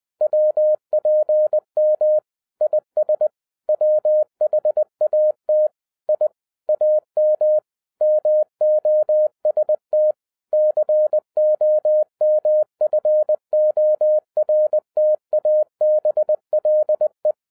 Morse code, heard as a keyed tone, about 20 wpm, 610 Hz. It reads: WPM IS WHAT I AM MOST COMFORTABLE
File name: 2011-07-mp3 / n08 A LOT HAS CHANGED 20wpm